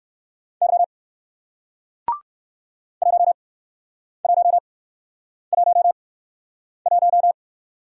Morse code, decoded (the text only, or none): Y 1111